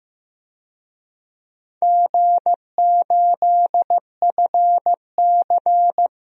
Morse code, heard as a keyed tone, about 15 wpm, 700 Hz. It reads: G8FC
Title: DXCC-100 Call Signs - 4 Characters - 15wpm